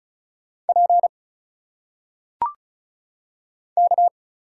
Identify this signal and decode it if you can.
Morse code, heard as a keyed tone, about 34 wpm, 700 Hz. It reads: P K
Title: Single Letters - Phonetic Pronunciation 35wpm